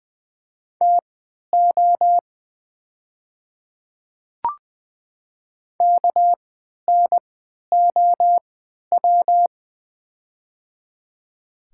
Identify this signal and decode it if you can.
Morse code, heard as a keyed tone, about 20 wpm, spaced as for 11 wpm, 690 Hz. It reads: TO KNOW
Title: 3x Character Spacing - Top 200 Words - 20wpm